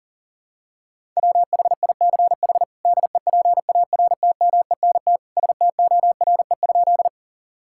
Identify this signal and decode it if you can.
Morse code, heard as a keyed tone, about 40 wpm, 700 Hz. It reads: WHICH DEPARTMENT STORE?
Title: Sentences from Top 1000 Words - Review Part 5 - 40wpm